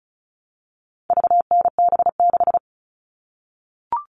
Morse code, heard as a keyed tone, about 34 wpm, 700 Hz. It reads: VNB6
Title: All Cut Numbers - 4-Digit 35wpm